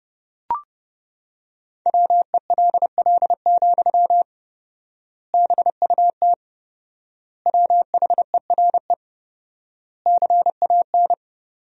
Morse code, heard as a keyed tone, about 29 wpm, 700 Hz. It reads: WELL, BUT WHERE CAN
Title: 4x Word-Spacing - Sentences from Top 500 Words - 30wpm